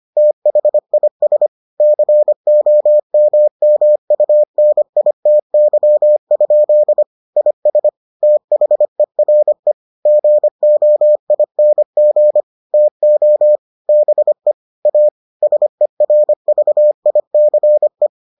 Morse code, heard as a keyed tone, about 25 wpm, 600 Hz. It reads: THIS COMMUNITY? IS THERE GOING TO BE A SERVICE